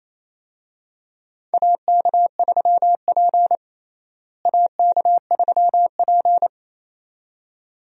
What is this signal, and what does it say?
Morse code, heard as a keyed tone, about 28 wpm, 700 Hz. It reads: AK3P AK3P
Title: USA Call Signs - 4 Characters - Repeated 2x - 28wpm